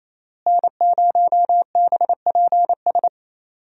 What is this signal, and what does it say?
Morse code, heard as a keyed tone, about 28 wpm, 700 Hz. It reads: N0BPS
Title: USA Call Signs - 5 Characters - Repeated 2x - 28wpm